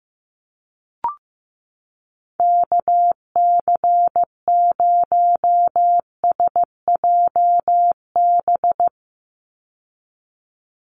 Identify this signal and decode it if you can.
Morse code, heard as a keyed tone, about 15 wpm, 700 Hz. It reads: KC0SJB
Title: USA Call Signs 15wpm